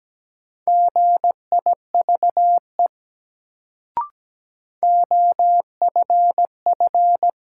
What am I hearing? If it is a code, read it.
Morse code, heard as a keyed tone, about 17 wpm, 700 Hz. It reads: GIVE OFF
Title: Sets of 2 Words - New Word Drill - 17wpm